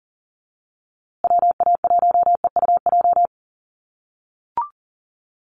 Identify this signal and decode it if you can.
Morse code, heard as a keyed tone, about 40 wpm, 700 Hz. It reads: WA1EUJ
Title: USA Call Signs Repeated 3x 40wpm